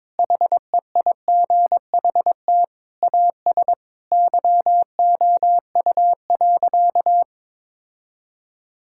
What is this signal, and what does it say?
Morse code, heard as a keyed tone, about 21 wpm, 700 Hz. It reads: HEIGHT AS YOU.
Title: Sentences from Top 1600 Words 22wpm